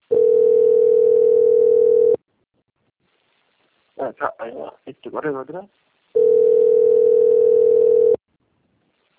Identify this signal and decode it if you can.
Ringback tone (North America), 440 + 480 Hz, with speech.
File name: out- - - - - .10465